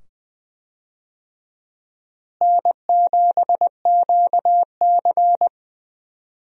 Morse code, heard as a keyed tone, about 20 wpm, 700 Hz. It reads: N7QC
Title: USA Call Signs - 4 Characters - Repeated 2x - 20wpm